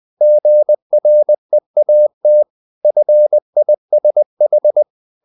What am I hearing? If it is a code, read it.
Morse code, heard as a keyed tone, about 20 wpm, 600 Hz. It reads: GREAT FISH
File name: the old man and the sea / the old man and the sea part 1 20wpm